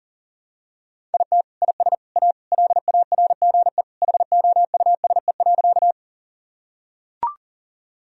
Morse code, heard as a keyed tone, about 37 wpm, 700 Hz. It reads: IT IS A LARGE HOUSE.
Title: Sentences from Top 1000 Words - Review Part 1 - 40wpm